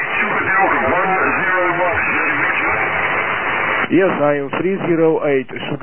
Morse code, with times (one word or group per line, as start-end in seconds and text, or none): none